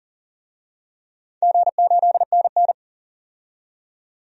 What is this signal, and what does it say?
Morse code, heard as a keyed tone, about 40 wpm, 700 Hz. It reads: G8NN
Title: DXCC-100 Call Signs - 4 Characters - 40wpm